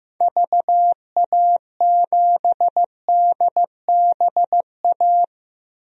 Morse code, heard as a keyed tone, about 14 wpm, 700 Hz. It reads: VA7DBA
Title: DXCC-100 Call Signs - 6 Characters - 15wpm